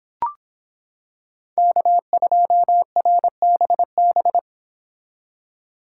Morse code, heard as a keyed tone, about 26 wpm, 700 Hz. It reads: K2RBB